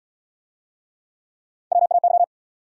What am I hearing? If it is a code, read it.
Morse code, heard as a keyed tone, about 69 wpm, 700 Hz. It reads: WAY